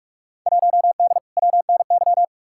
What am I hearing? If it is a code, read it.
Morse code, heard as a keyed tone, about 46 wpm, 700 Hz. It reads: 1D WNY
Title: Field Day Exchanges - 45wpm